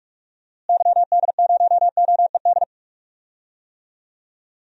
Morse code, heard as a keyed tone, about 42 wpm, 700 Hz. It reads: YD0OED
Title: DXCC-100 Call Signs - 6 Characters - Repeated 3x - 45wpm